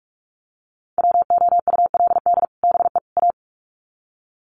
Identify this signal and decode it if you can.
Morse code, heard as a keyed tone, about 44 wpm, 700 Hz. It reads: WOULD BE A
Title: Sets of 3 Words 45wpm